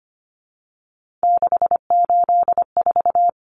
Morse code, heard as a keyed tone, about 25 wpm, 700 Hz. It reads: 684
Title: Numbers - 3-Digit 25wpm